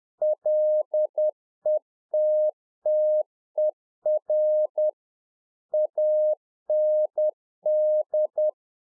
Morse code, heard as a keyed tone, about 10 wpm, 610 Hz. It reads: LETTER AND
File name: UNFOGETTABLE QSO 10WPM